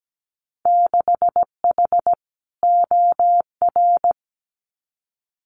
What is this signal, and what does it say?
Morse code, heard as a keyed tone, about 17 wpm, 700 Hz. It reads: 6H OR
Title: Winter Field Day Exchanges Encore 17wpm